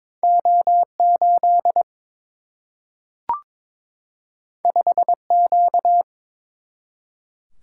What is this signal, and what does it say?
Morse code, heard as a keyed tone, about 22 wpm, 700 Hz. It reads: O 8 5 Q
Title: Single-Letter Single-Number 22wpm